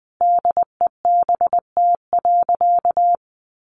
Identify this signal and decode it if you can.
Morse code, heard as a keyed tone, about 20 wpm, 700 Hz. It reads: DEBT.
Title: Sentences from Top 1300 Words 20wpm